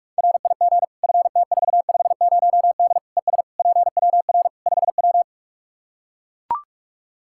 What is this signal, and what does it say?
Morse code, heard as a keyed tone, about 43 wpm, 700 Hz. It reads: RIG FT450D ES PWR 5W
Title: Practice QSO - Exchange 2 - Info - 45wpm